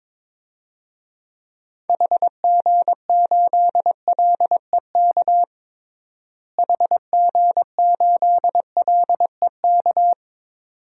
Morse code, heard as a keyed tone, about 21 wpm, 700 Hz. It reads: HG8LEK HG8LEK
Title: DXCC-100 Call Signs - 6 Characters - Repeated 2x - 22wpm